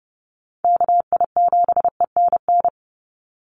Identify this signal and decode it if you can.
Morse code, heard as a keyed tone, about 30 wpm, 700 Hz. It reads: KI7ENN
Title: USA Call Signs 30wpm